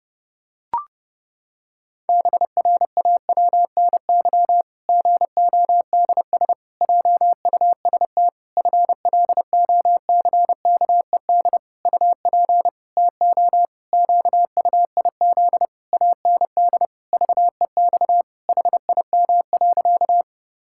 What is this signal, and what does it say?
Morse code, heard as a keyed tone, about 32 wpm, 700 Hz. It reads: BRAWNY GODS JUST FLOCKED UP TO QUIZ AND VEX HIM.
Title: Pangram Challenge v2 - 30wpm